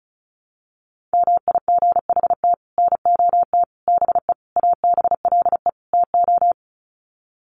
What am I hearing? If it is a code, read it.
Morse code, heard as a keyed tone, about 34 wpm, 700 Hz. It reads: MIGHT NOT BE ABLE TO
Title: Sets of 5 Words Encore2 35wpm